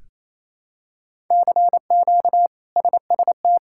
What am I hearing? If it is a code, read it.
Morse code, heard as a keyed tone, about 27 wpm, 700 Hz. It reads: CQ SST
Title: K1USN SST QSO Practice - 28wpm